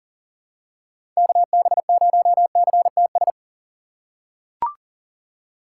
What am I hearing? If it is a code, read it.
Morse code, heard as a keyed tone, about 40 wpm, 700 Hz. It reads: KB0CTS
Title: USA Call Signs - 6 Characters - 2x Character Spacing - 40wpm